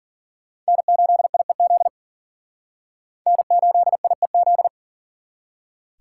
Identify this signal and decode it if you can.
Morse code, heard as a keyed tone, about 52 wpm, 700 Hz. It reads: N8IEZ N8IEZ